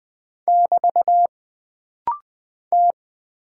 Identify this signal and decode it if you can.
Morse code, heard as a keyed tone, about 20 wpm, 700 Hz. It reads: <BT> T
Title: Single Character (Letter:Number:Punctuation:Prosign) - Rapid-Fire - 20wpm